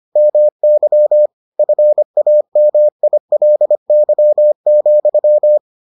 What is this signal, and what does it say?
Morse code, heard as a keyed tone, about 25 wpm, 600 Hz. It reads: MY FAMILY,